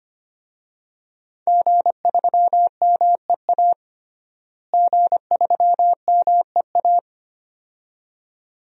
Morse code, heard as a keyed tone, about 25 wpm, 700 Hz. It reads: G3MEA G3MEA
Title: DXCC-100 Call Signs - 5 Characters - Repeated 2x - 25wpm